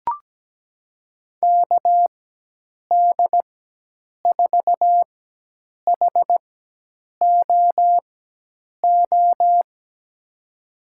Morse code, heard as a keyed tone, about 17 wpm, spaced as for 8 wpm, 700 Hz. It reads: KD4HOO
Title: USA Call Signs - 6 Characters - 4x Character Spacing - 17wpm